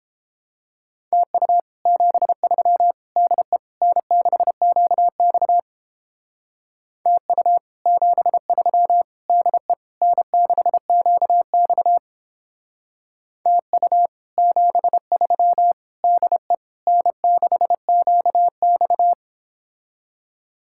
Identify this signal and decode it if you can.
Morse code, heard as a keyed tone, about 29 wpm, 700 Hz. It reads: TU 73 DE N6QX TU 73 DE N6QX TU 73 DE N6QX